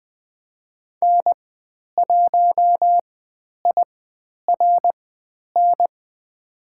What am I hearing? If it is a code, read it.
Morse code, heard as a keyed tone, about 20 wpm, spaced as for 10 wpm, 700 Hz. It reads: N1IRN
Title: 109 Lesson 18g - Letter m - Call Signs 20wpm